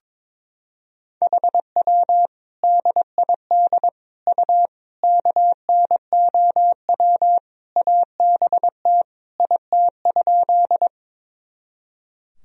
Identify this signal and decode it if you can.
Morse code, heard as a keyed tone, about 22 wpm, 700 Hz. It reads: HW DID U KNOW ABT IT?
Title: Sentences from Top 100 Words and Abbreviations 22wpm